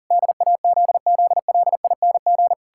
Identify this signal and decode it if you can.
Morse code, heard as a keyed tone, about 40 wpm, 700 Hz. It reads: DAZZLING
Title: Pangram Challenge v2 - 40wpm